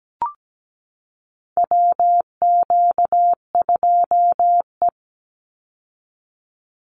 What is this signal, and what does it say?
Morse code, heard as a keyed tone, about 17 wpm, 700 Hz. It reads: WQ2E
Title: Call Signs Repeated 2x - 17wpm